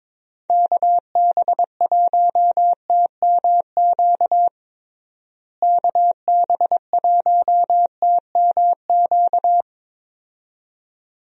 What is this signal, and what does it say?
Morse code, heard as a keyed tone, about 22 wpm, 700 Hz. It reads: KB1TMQ KB1TMQ
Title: USA Call Signs - 6 Characters - Repeated 2x - 22wpm